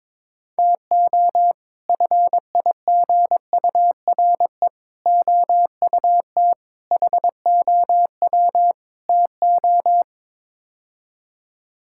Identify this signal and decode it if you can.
Morse code, heard as a keyed tone, about 22 wpm, 700 Hz. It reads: TO FIGURE OUT HOW TO
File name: practice / 22wpm / Build A Phrase - 5-Word Phrases - 22wpm